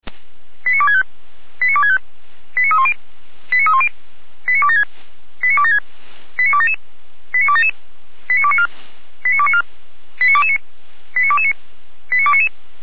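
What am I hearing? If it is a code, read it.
Five-tone selective call (ZVEI), heard as quick runs of five tones, tones of about 70 ms. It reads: 89265 89265 89310 89310 89276 89276 89277 89277 89294 89294 89229 89229 89299
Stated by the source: Received by SDR on a ZVEI selective-call channel.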